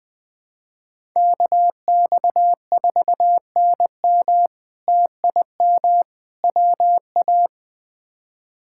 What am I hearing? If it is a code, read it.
Morse code, heard as a keyed tone, about 20 wpm, 700 Hz. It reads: KX4NM TIM WA
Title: K1USN SST QSO Practice - 20wpm